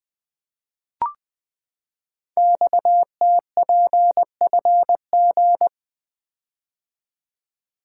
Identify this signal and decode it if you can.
Morse code, heard as a keyed tone, about 20 wpm, 700 Hz. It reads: XTPFG
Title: Code Groups - 5 Letters - 20wpm